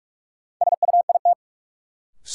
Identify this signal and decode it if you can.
Morse code, heard as a keyed tone, about 45 wpm, 700 Hz. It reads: SUIT
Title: Sentences from Top 1600 Words - New Word Drill Alt - 45wpm